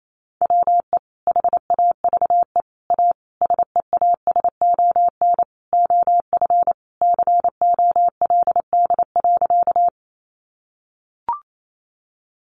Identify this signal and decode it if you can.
Morse code, heard as a keyed tone, about 28 wpm, 700 Hz. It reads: WE HAVE A SEASON OF COLD.
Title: Sentences from Top 1400 Words 28wpm